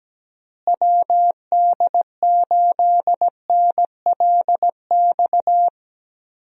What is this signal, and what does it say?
Morse code, heard as a keyed tone, about 17 wpm, 700 Hz. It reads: WD8NLX
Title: USA Call Signs - 6 Characters - Repeated 2x - 17wpm